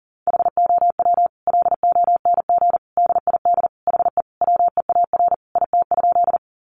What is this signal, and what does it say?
Morse code, heard as a keyed tone, about 40 wpm, 700 Hz. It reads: HOW LONG DID HE WEAR IT?
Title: Sentences from Top 1500 Words 40wpm